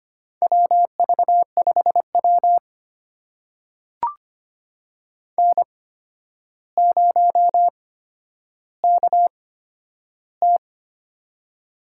Morse code, heard as a keyed tone, about 25 wpm, spaced as for 11 wpm, 700 Hz. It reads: WV5W N0KT
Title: USA Call Signs - 4 Characters - 8x Character Spacing - 25wpm